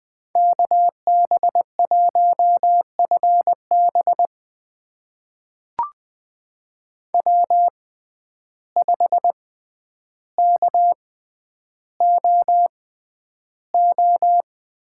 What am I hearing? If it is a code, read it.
Morse code, heard as a keyed tone, about 20 wpm, spaced as for 10 wpm, 700 Hz. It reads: KB1FB W5KOO
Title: USA Call Signs - 5 Characters - 6x Character Spacing - 20wpm